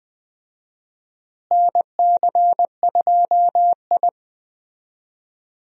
Morse code, heard as a keyed tone, about 20 wpm, 700 Hz. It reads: NC2I
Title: USA Call Signs - 4 Characters - Repeated 2x - 20wpm